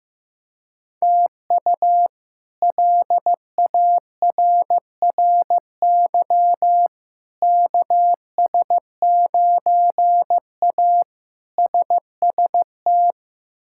Morse code, heard as a keyed tone, about 16 wpm, 700 Hz. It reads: TU LARRY KS9A SST
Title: K1USN SST QSO Practice - 15wpm